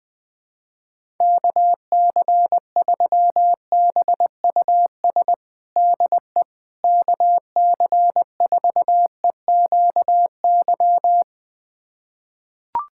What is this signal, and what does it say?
Morse code, heard as a keyed tone, about 20 wpm, 700 Hz. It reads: KC3BUS DE KC4EQY
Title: USA Callsign DE Callsign 20wpm